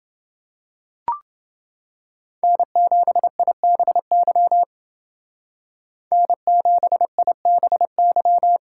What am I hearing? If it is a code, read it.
Morse code, heard as a keyed tone, about 29 wpm, 700 Hz. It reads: N7IBY N7IBY